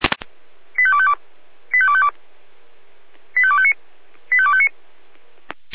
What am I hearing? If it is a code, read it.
Five-tone selective call (ZVEI), heard as quick runs of five tones, tones of about 70 ms. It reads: 85252 85252 85259 85259